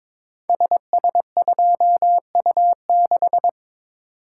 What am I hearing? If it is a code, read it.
Morse code, heard as a keyed tone, about 22 wpm, 690 Hz. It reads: SS2U6
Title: Code Groups - 5 Letter:Numbers - 22wpm